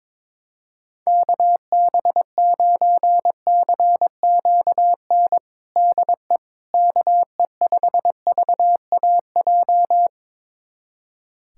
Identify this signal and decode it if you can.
Morse code, heard as a keyed tone, about 22 wpm, 700 Hz. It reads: KB9CQN DE KE5VAJ
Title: USA Callsign DE Callsign - 6 Characters - 22wpm